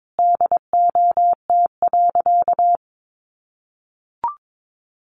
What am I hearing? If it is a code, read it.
Morse code, heard as a keyed tone, about 22 wpm, 700 Hz. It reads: DOT.
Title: Sentences from Top 1500 Words 22wpm